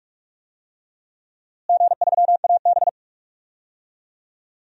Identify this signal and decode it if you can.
Morse code, heard as a keyed tone, about 46 wpm, 700 Hz. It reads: G3AB
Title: DXCC-100 Call Signs - 4 Characters - Repeated 3x - 45wpm